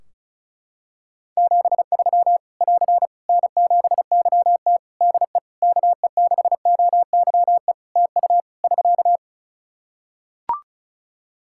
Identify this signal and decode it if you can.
Morse code, heard as a keyed tone, about 36 wpm, 700 Hz. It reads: 73 <AR> N7YT DE KE6OYE TU <SK>
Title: Practice QSO - Exchange 4 - Ending - 35wpm